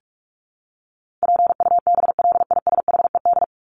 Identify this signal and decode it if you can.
Morse code, heard as a keyed tone, about 44 wpm, 700 Hz. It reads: PUBLISHED
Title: ED Suffix - Top 100 Words with Base Words 45wpm